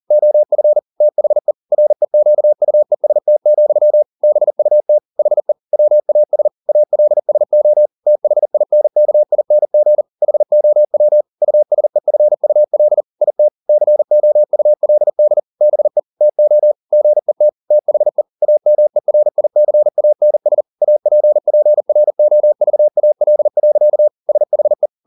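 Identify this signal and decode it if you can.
Morse code, heard as a keyed tone, about 40 wpm, 600 Hz. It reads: OF THE REQUEST, BUT HE WAS ALSO THINKING HOW USEFUL IT COULD BE TO GET THE AMERICANS APPROVAL. SHE
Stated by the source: Morse code at 40 wpm, practice text, CW med Gnister.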